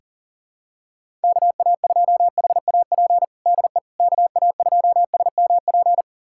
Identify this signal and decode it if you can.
Morse code, heard as a keyed tone, about 40 wpm, 700 Hz. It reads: KA2HAP DE KA2SMP
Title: USA Callsign DE Callsign - 6 Characters - 40wpm